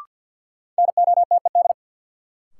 Morse code, heard as a keyed tone, about 54 wpm, 700 Hz. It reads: NOTED
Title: Sets of 5 Words - New Word Drill - 50wpm